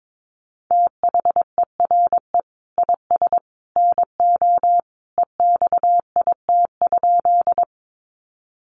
Morse code, heard as a keyed tone, about 22 wpm, 700 Hz. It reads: THERE IS NO EXIT?
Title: Sentences from Top 1100 Words 22wpm